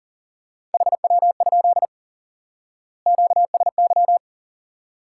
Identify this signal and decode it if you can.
Morse code, heard as a keyed tone, about 37 wpm, 700 Hz. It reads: HW? QSY
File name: IQE Course 3 - Lesson 89 - YAGI - Two Elements - 40wpm